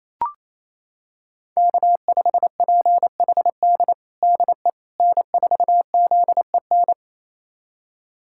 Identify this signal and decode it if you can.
Morse code, heard as a keyed tone, about 28 wpm, 700 Hz. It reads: K5PHD DE N4ZEN
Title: USA Callsign DE Callsign - 5 Characters - 28wpm